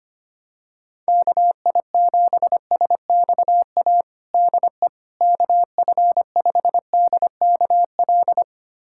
Morse code, heard as a keyed tone, about 25 wpm, 700 Hz. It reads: KI7SXA DE KF5DKL
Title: USA Callsign DE Callsign - 6 Characters - 25wpm